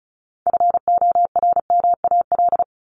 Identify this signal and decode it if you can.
Morse code, heard as a keyed tone, about 34 wpm, 700 Hz. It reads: FORMAL
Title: Top 1251-1500 Words 35wpm